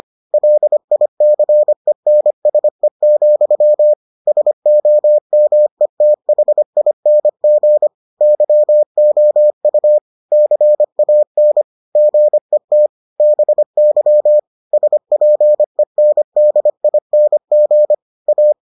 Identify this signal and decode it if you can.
Morse code, heard as a keyed tone, about 25 wpm, 600 Hz. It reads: LICENSE, SOMETHING YOU CAN GET BY SPENDING A